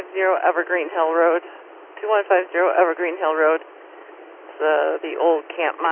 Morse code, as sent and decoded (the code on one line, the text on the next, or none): none